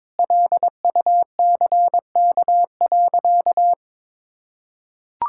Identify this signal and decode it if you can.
Morse code, heard as a keyed tone, about 22 wpm, 700 Hz. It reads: LUCK.
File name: Sentences with 7-Word Phrases - 22wpm